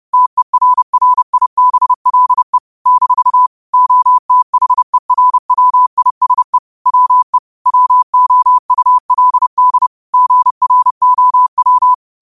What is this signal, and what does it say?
Morse code, heard as a keyed tone, about 30 wpm, 1000 Hz. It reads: TERRIBLE - OTHERWISE WE WOULD GROW